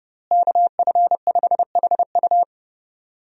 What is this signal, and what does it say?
Morse code, heard as a keyed tone, about 30 wpm, 700 Hz. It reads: KF5HU
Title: USA Call Signs - 5 Characters - Repeated 2x - 30wpm